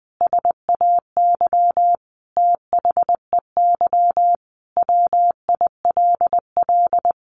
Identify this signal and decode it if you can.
Morse code, heard as a keyed tone, about 20 wpm, 700 Hz. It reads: SAY THEY WILL
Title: Sentences from Top 1100 Words 20wpm